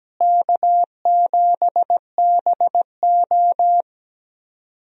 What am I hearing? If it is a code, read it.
Morse code, heard as a keyed tone, about 17 wpm, 700 Hz. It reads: K7BO
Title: USA Call Signs - 4 Characters - Repeated 3x - 17wpm